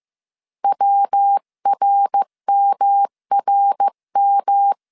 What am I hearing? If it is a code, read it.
Morse code, heard as a keyed tone, about 15 wpm, 800 Hz. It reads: WRMRM
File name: CW / DET / 1 3 1500C S A 14wpm 2447